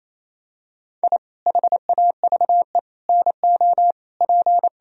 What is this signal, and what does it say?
Morse code, heard as a keyed tone, about 28 wpm, 700 Hz. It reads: I HAVE NO P
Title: Sentences from Top 1000 Words - Review Part 4 - 28wpm